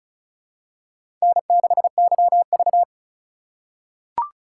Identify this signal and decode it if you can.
Morse code, heard as a keyed tone, about 36 wpm, 700 Hz. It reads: N6YV
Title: USA Call Signs - 4 Characters - 2x Character Spacing - 35wpm